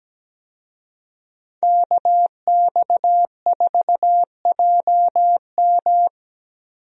Morse code, heard as a keyed tone, about 17 wpm, 700 Hz. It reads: KX4JM
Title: USA Call Signs - 5 Characters - Repeated 3x - 17wpm